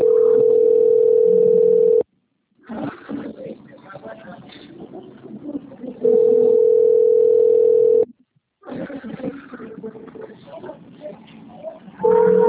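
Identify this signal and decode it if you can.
Ringback tone (North America), 440 + 480 Hz.
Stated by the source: Outgoing call recorded at the Valinhos - SP call system.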